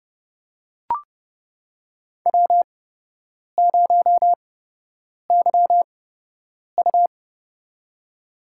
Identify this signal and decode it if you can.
Morse code, heard as a keyed tone, about 30 wpm, spaced as for 8 wpm, 700 Hz. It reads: W0YU
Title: USA Call Signs - 4 Characters - 8x Character Spacing - 30wpm